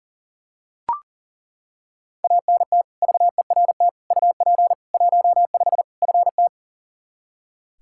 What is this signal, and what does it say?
Morse code, heard as a keyed tone, about 40 wpm, 700 Hz. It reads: ANT VERT UP 15 FT
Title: Practice QSO - Exchange 2 - Info - 40wpm